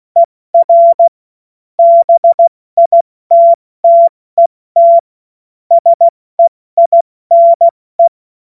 Morse code, heard as a keyed tone, about 16 wpm, spaced as for 14 wpm, 680 Hz. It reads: ER BITTET SEINE
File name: Morsekurs / Witze / Witz 12 März 2024 70 F80